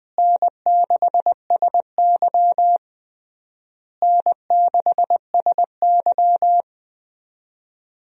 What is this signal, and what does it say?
Morse code, heard as a keyed tone, about 21 wpm, 700 Hz. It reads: N6SY N6SY
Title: USA Call Signs - 4 Characters - Repeated 3x - 20wpm